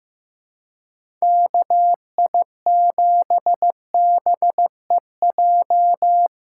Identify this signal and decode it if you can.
Morse code, heard as a keyed tone, about 15 wpm, 700 Hz. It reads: KI7BEJ
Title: USA Callsign DE Callsign - 6 Characters - 15wpm